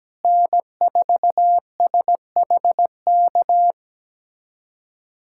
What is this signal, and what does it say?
Morse code, heard as a keyed tone, about 17 wpm, 700 Hz. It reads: N4SHK
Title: USA Callsign DE Callsign 17wpm